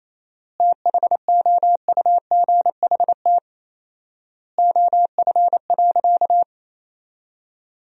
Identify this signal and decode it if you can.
Morse code, heard as a keyed tone, about 27 wpm, 700 Hz. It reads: THOUGHT OF.
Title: 4x Word-Spacing - Sentences from Top 300 Words - 28wpm